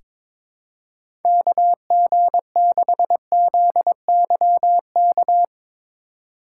Morse code, heard as a keyed tone, about 22 wpm, 700 Hz. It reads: KG6ZYK
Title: USA Call Signs - 3x Character Spacing - 22wpm